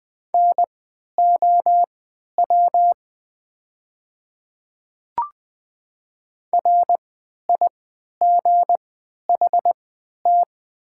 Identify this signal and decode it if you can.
Morse code, heard as a keyed tone, about 20 wpm, spaced as for 11 wpm, 690 Hz. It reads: NOW RIGHT
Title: 3x Character Spacing - Top 200 Words - 20wpm